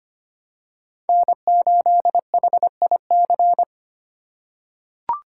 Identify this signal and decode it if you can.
Morse code, heard as a keyed tone, about 25 wpm, 700 Hz. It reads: N8HIC
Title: USA Call Signs - 5 Characters - 3x Character Spacing - 25wpm